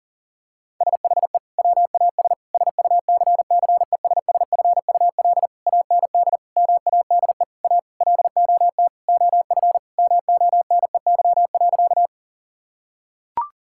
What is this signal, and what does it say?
Morse code, heard as a keyed tone, about 40 wpm, 700 Hz. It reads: SHE WAS SUCCESSFUL AND MADE A LOT OF MONEY.
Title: Sentences from Top 1000 Words - Review Part 6 - 40wpm